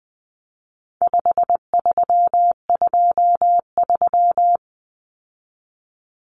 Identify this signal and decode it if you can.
Morse code, heard as a keyed tone, about 20 wpm, 700 Hz. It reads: 5323
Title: Numbers - 4-Digit 20wpm